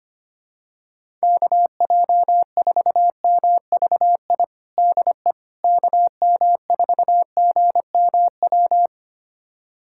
Morse code, heard as a keyed tone, about 25 wpm, 700 Hz. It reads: KJ4MVI DE KM4GMW
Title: USA Callsign DE Callsign - 6 Characters - 25wpm